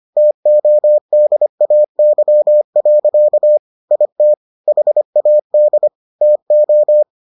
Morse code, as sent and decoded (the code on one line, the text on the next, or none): - --- -.. .- -.-- .-.-.- / .. - / .... .- -.. / - ---
TODAY. IT HAD TO